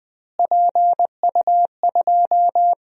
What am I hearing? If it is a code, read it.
Morse code, heard as a keyed tone, about 20 wpm, 700 Hz. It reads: PU2
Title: DXCC-100 Call Signs - 6 Characters - Repeated 3x - 20wpm